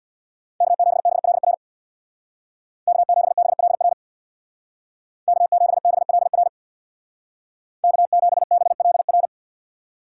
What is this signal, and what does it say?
Morse code, heard as a keyed tone, about 64 wpm, 700 Hz. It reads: K7BLR K7BLR K7BLR K7BLR